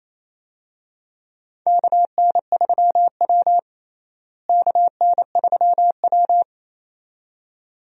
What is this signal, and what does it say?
Morse code, heard as a keyed tone, about 28 wpm, 700 Hz. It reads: KN3W KN3W
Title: USA Call Signs - 4 Characters - Repeated 2x - 28wpm